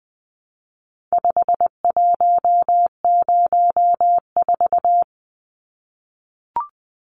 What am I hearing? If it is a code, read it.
Morse code, heard as a keyed tone, about 20 wpm, 700 Hz. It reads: 5104
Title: Numbers - 4-Digit 20wpm